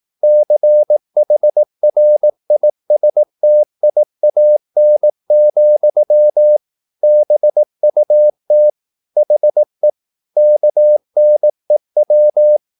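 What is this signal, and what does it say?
Morse code, heard as a keyed tone, about 18 wpm, 600 Hz. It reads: CHRISTIAN, BUT HE KNEW
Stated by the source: Morse practice at 18 wpm, English text, from CW med Gnister.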